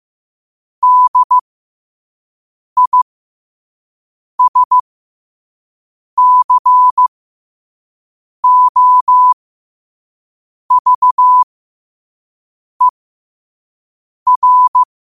Morse code, heard as a keyed tone, about 15 wpm, spaced as for 5 wpm, 1000 Hz. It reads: DISCOVER